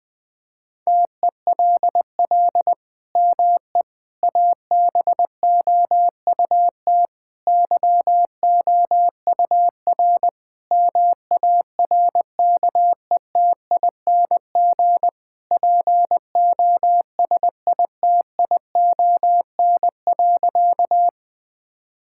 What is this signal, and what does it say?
Morse code, heard as a keyed tone, about 19 wpm, 700 Hz. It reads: TELL ME ABOUT YOUR MARKETING POSITION.
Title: Sentences from Top 1000 Words - Review Part 5 - 20wpm